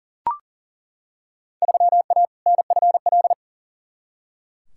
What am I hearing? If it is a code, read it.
Morse code, heard as a keyed tone, about 40 wpm, 700 Hz. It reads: 3A NFL
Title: Field Day Exchanges - 40wpm